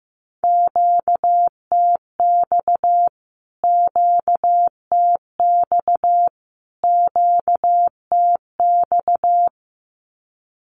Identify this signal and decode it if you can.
Morse code, heard as a keyed tone, about 15 wpm, 700 Hz. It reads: QTX QTX QTX
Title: Q-Codes Repeated 3x 15wpm